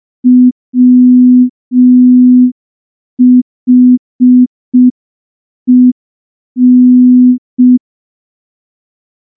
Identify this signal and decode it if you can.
Morse code, heard as a keyed tone, about 5 wpm, 250 Hz. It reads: WHEN
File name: morse / morse-training-05WPM